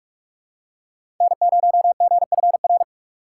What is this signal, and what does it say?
Morse code, heard as a keyed tone, about 42 wpm, 700 Hz. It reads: N0GFR
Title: USA Call Signs - 5 Characters - 4x Character Spacing - 45wpm